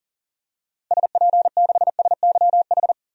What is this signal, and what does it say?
Morse code, heard as a keyed tone, about 40 wpm, 700 Hz. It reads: SP6SYH
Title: DXCC-100 Call Signs - 6 Characters - 40wpm